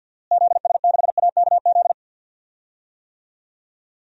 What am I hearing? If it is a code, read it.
Morse code, heard as a keyed tone, about 50 wpm, 700 Hz. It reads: ZS6AKZ
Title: DXCC-100 Call Signs - 6 Characters - Repeated 3x - 50wpm